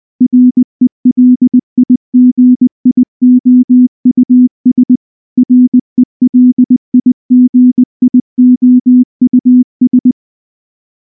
Morse code, heard as a keyed tone, about 19 wpm, 250 Hz. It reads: RELIGIOUS RELIGIOUS
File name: morse / morse-training-20WPM-6